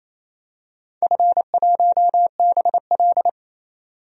Morse code, heard as a keyed tone, about 27 wpm, 700 Hz. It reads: F1BL
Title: DXCC-100 Call Signs - 4 Characters - 28wpm